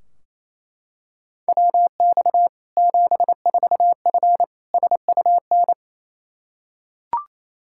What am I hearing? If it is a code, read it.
Morse code, heard as a keyed tone, about 27 wpm, 700 Hz. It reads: WX 74F SUN
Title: Practice QSO - Exchange 2 - Info - 28wpm